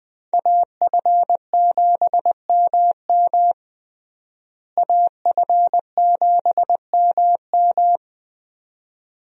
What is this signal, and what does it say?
Morse code, heard as a keyed tone, about 21 wpm, 700 Hz. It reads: AF7MM AF7MM
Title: USA Call Signs - 5 Characters - Repeated 2x - 20wpm